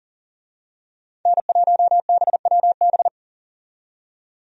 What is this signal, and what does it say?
Morse code, heard as a keyed tone, about 40 wpm, 700 Hz. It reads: N1BWB
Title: USA Call Signs - 5 Characters - 40wpm